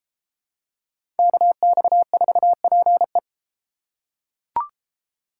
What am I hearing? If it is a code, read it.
Morse code, heard as a keyed tone, about 29 wpm, 700 Hz. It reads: KX4PE